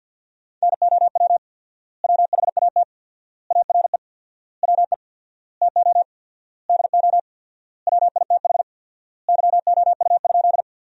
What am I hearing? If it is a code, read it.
Morse code, heard as a keyed tone, about 47 wpm, 700 Hz. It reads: NOW WHAT ARE WE TO DO WITH YOU?
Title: 4x Word-Spacing - Sentences from Top 100 Words - 50wpm